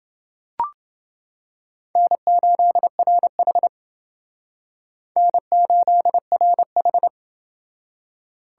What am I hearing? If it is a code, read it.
Morse code, heard as a keyed tone, about 28 wpm, 700 Hz. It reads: N8RH N8RH